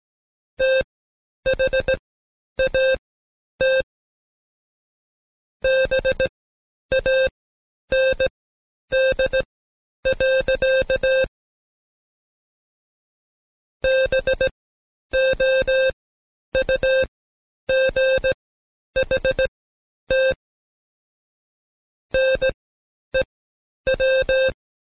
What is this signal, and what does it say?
Morse code, heard as a keyed tone, about 17 wpm, spaced as for 9 wpm, 530 Hz. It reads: THAT BAND. BOUGHT NEW
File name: Code Quick / AudioDownld2 / Seg131-Stepper4 9WPM